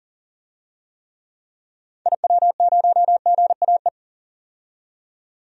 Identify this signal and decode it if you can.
Morse code, heard as a keyed tone, about 40 wpm, 700 Hz. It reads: IW0GAE
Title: DXCC-100 Call Signs - 6 Characters - 40wpm